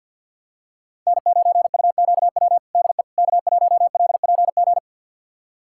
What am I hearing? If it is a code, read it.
Morse code, heard as a keyed tone, about 51 wpm, 700 Hz. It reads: N9UQW DE K1LPG
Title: USA Callsign DE Callsign - 5 Characters - 50wpm